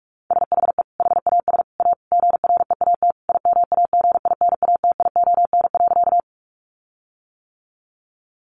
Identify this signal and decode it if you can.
Morse code, heard as a keyed tone, about 44 wpm, 700 Hz. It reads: SHE HAS A GREAT IMAGINATION.
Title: Sentences from Top 1300 Words 45wpm